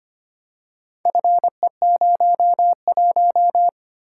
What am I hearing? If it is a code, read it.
Morse code, heard as a keyed tone, about 25 wpm, 700 Hz. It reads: FE01
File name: practice / 25wpm / Four Letter-Number Combinations - 25wpm